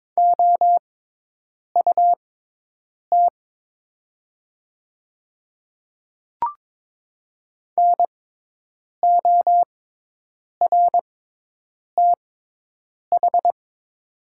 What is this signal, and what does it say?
Morse code, heard as a keyed tone, about 22 wpm, spaced as for 8 wpm, 700 Hz. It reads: OUT NORTH
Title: 6x Character Spacing - Top 300 Words - 22wpm